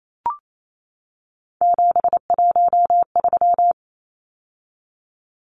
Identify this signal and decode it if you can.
Morse code, heard as a keyed tone, about 28 wpm, 700 Hz. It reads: 713
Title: Numbers - 3-Digit 28wpm